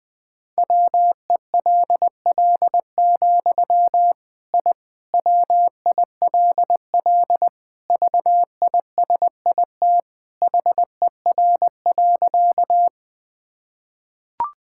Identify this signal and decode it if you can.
Morse code, heard as a keyed tone, about 21 wpm, 700 Hz. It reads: WELL, I WILL VISIT HER.
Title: Sentences from Top 1000 Words - Review Part 6 - 20wpm